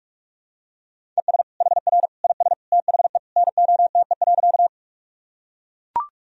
Morse code, heard as a keyed tone, about 44 wpm, 700 Hz. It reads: ES HR IS THE NOTE.
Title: Sentences from Top 500 Words and Abbreviations 45wpm